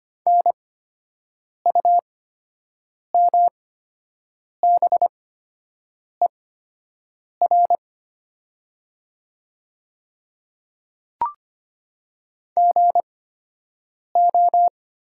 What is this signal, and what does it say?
Morse code, heard as a keyed tone, about 25 wpm, spaced as for 7 wpm, 700 Hz. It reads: NUMBER GO
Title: 8x Character Spacing - Top 100 Words - 25wpm